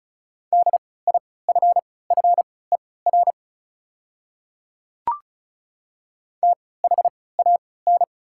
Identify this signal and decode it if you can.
Morse code, heard as a keyed tone, about 35 wpm, spaced as for 20 wpm, 700 Hz. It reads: DIFFER THAN
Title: 3x Character Spacing - Top 200 Words - 35wpm